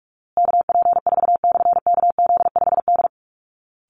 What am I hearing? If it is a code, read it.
Morse code, heard as a keyed tone, about 44 wpm, 700 Hz. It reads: KP4/KZ5D
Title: Call Signs Repeated 3x - 45wpm